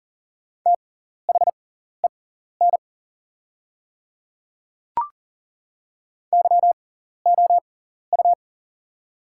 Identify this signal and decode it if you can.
Morse code, heard as a keyed tone, about 40 wpm, spaced as for 14 wpm, 690 Hz. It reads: THEN YOU
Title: 6x Character Spacing - Top 100 Words - 40wpm